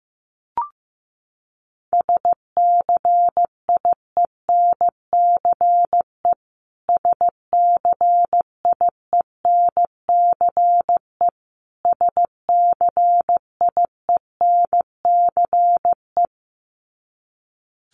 Morse code, heard as a keyed tone, about 15 wpm, 700 Hz. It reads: SCIENCE SCIENCE SCIENCE
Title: Top 500 Words Repeated 15wpm